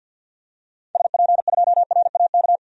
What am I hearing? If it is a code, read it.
Morse code, heard as a keyed tone, about 52 wpm, 700 Hz. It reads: SP2RAK
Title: DXCC-100 Call Signs - 6 Characters - 50wpm